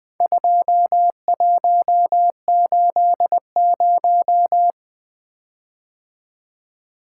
Morse code, heard as a keyed tone, about 20 wpm, 700 Hz. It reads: 2180
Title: Four Letter-Number Combinations - 20wpm